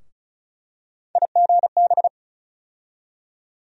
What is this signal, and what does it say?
Morse code, heard as a keyed tone, about 36 wpm, 700 Hz. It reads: IGB
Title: USA Call Sign Postfixes - 3 Letters 35wpm